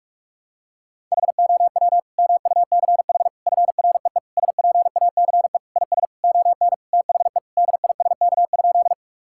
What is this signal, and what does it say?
Morse code, heard as a keyed tone, about 46 wpm, 700 Hz. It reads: HOW MUCH FREE SPACE IS ON THE DISK?
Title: Sentences from Top 1000 Words - Review Part 6 - 45wpm